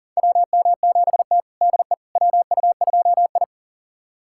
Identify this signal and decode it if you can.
Morse code, heard as a keyed tone, about 37 wpm, 700 Hz. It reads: WM7T DE WU2I
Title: USA Callsign DE Callsign - 4 Characters - 40wpm